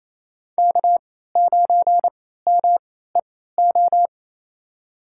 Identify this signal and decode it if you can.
Morse code, heard as a keyed tone, about 28 wpm, spaced as for 16 wpm, 700 Hz. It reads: K9MEO
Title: USA Call Signs - 3x Character Spacing - 28wpm